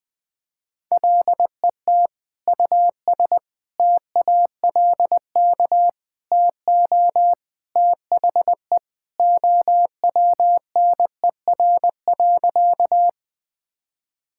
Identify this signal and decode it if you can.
Morse code, heard as a keyed tone, about 21 wpm, 700 Hz. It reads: LET US TALK TO THE OWNER.
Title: Sentences from Top 1600 Words 20wpm